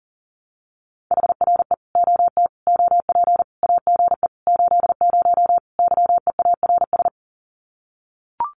Morse code, heard as a keyed tone, about 40 wpm, 700 Hz. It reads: HRE OT OP AGE 80 YEARS
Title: Common QSO Phrases 40wpm